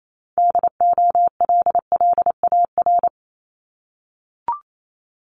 Morse code, heard as a keyed tone, about 28 wpm, 700 Hz. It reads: DOLLAR
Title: Multi-letter Phonemes 28wpm